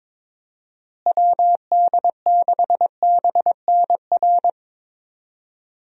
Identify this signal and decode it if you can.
Morse code, heard as a keyed tone, about 22 wpm, 700 Hz. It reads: WD6BNR
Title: USA Call Signs - 6 Characters - Repeated 3x - 22wpm